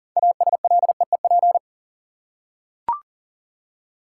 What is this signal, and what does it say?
Morse code, heard as a keyed tone, about 40 wpm, 700 Hz. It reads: ASLEEP
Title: Six Letter Words - 40wpm